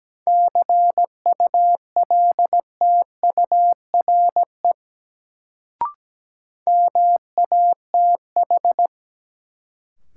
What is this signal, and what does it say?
Morse code, heard as a keyed tone, about 17 wpm, 700 Hz. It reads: CULTURE MATH
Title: Sentences from Top 1000 Words - Review Part 6 - New Word Drill - 17wpm